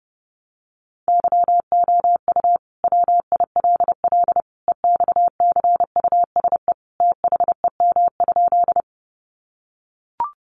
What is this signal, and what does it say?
Morse code, heard as a keyed tone, about 30 wpm, 700 Hz. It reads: YOU WILL EXCUSE THEM?
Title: Sentences from Top 700 Words 30wpm